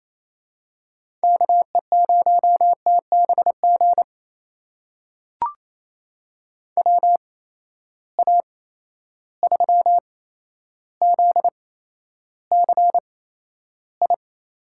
Morse code, heard as a keyed tone, about 28 wpm, spaced as for 12 wpm, 700 Hz. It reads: KE0TBG WA3ZCI
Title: USA Call Signs - 6 Characters - 8x Character Spacing - 28wpm